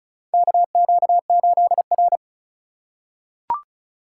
Morse code, heard as a keyed tone, about 36 wpm, 700 Hz. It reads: KQ8R
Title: USA Call Signs - 4x Character Spacing - 35wpm